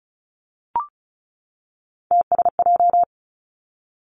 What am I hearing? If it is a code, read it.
Morse code, heard as a keyed tone, about 36 wpm, 700 Hz. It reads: TSJ
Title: Three Challenging Letter-Number Combinations 35wpm